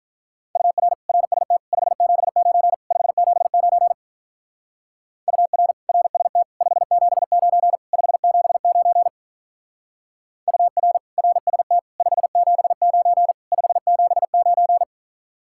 Morse code, heard as a keyed tone, about 44 wpm, 700 Hz. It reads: UR RST 579 579 UR RST 579 579 UR RST 579 579